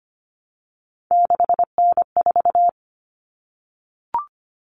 Morse code, heard as a keyed tone, about 25 wpm, 700 Hz. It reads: 6N4
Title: Common Cut Numbers - 3-Digit 25wpm